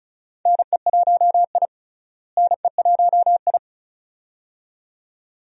Morse code, heard as a keyed tone, about 34 wpm, 700 Hz. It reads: NE1I NE1I
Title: USA Call Signs - 4 Characters - Repeated 2x - 35wpm